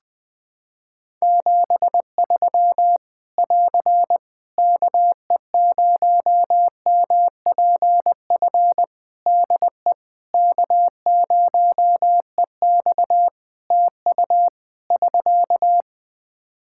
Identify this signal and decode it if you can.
Morse code, heard as a keyed tone, about 20 wpm, 700 Hz. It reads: 73 <AR> KE0MPF DE K0EX TU <SK>
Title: Practice QSO - Exchange 4 - Ending - 20wpm